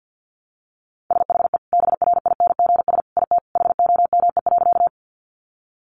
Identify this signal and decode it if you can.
Morse code, heard as a keyed tone, about 50 wpm, 700 Hz. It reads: SHE BRINGS IT HOME.
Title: Easy Phrases 50wpm